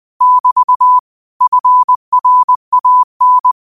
Morse code, heard as a keyed tone, about 20 wpm, 1000 Hz. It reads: <BT> FRAN
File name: qotdfiles / 20wpm